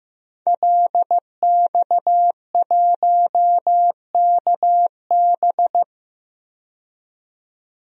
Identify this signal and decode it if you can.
Morse code, heard as a keyed tone, about 15 wpm, 700 Hz. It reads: LX1KB
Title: DXCC-100 Call Signs - 5 Characters - 15wpm